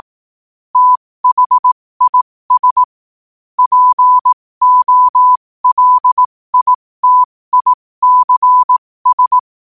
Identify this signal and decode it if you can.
Morse code, heard as a keyed tone, about 18 wpm, spaced as for 15 wpm, 1000 Hz. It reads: THIS POLITICS
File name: qotdfiles / 15wpm